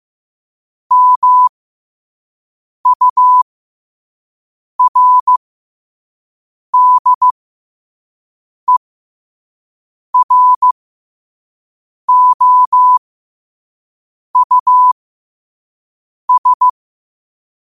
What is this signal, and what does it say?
Morse code, heard as a keyed tone, about 15 wpm, spaced as for 5 wpm, 1000 Hz. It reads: MURDEROUS
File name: qotdfiles / 5wpm